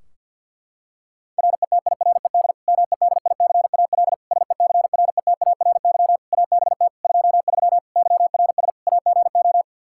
Morse code, heard as a keyed tone, about 52 wpm, 700 Hz. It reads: RETIRED MEDICAL SECRETARY ABT 23 YRS AGO
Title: K1USN SST QSO Practice 50wpm